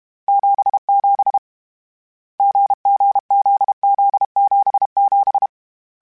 Morse code, heard as a keyed tone, about 32 wpm, 800 Hz. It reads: 77 GGZZ77
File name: CW Code Groups-mp3s / Sets mp3s / GZ GZ7 sp @30